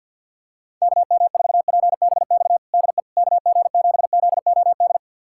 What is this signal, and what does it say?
Morse code, heard as a keyed tone, about 48 wpm, 700 Hz. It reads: KM4PBX DE KG7ZOD
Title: USA Callsign DE Callsign - 6 Characters - 50wpm